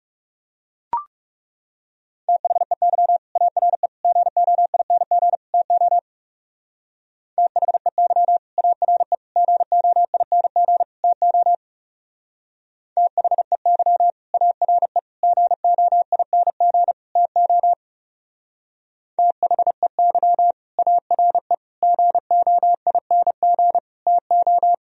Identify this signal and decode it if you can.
Morse code, heard as a keyed tone, about 34 wpm, 700 Hz. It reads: THEY ARE GOING TO THEY ARE GOING TO THEY ARE GOING TO THEY ARE GOING TO
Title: Speed-Racing - Sets of 4 Words 30wpm